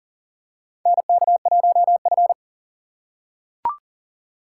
Morse code, heard as a keyed tone, about 40 wpm, 700 Hz. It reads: NK1F
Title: USA Call Signs - 4 Characters - 4x Character Spacing - 40wpm